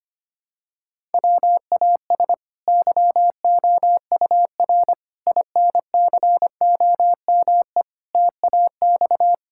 Morse code, heard as a keyed tone, about 25 wpm, 700 Hz. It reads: WAS YOUR INCOME TAX
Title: Sentences from Top 1000 Words - Review Part 5 - 25wpm